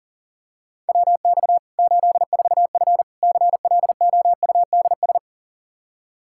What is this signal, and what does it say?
Morse code, heard as a keyed tone, about 40 wpm, 700 Hz. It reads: WX 84F CLOUDS
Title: Practice QSO - Exchange 2 - Info - 40wpm